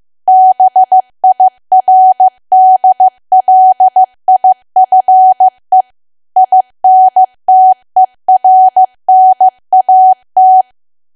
Morse code, heard as a keyed tone, about 15 wpm, 750 Hz. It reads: BIRDLIFE INTERNAT
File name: cwa / W1AW-Level2 / 15WPM